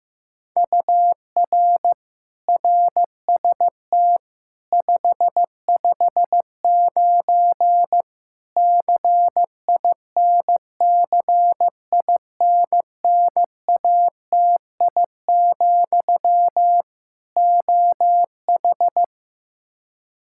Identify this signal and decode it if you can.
Morse code, heard as a keyed tone, about 15 wpm, 700 Hz. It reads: UR RST 559 CINCINNATI, OH
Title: Practice QSO - Exchange 1 - Beginning - 15wpm